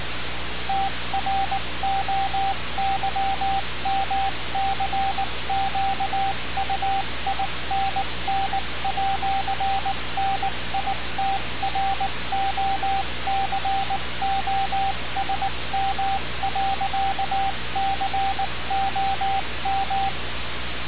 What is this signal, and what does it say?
Morse code, heard as a keyed tone, about 19 wpm, 780 Hz, with radio noise. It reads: TROYMCQUINN@NITROCOSM.COM
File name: radiocap / REC z